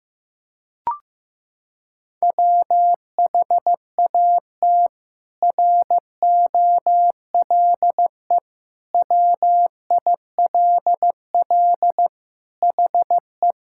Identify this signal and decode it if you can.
Morse code, heard as a keyed tone, about 15 wpm, 700 Hz. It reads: WHAT ROLE WILL HE
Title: Sentences from Top 1000 Words - Review Part 6 - 15wpm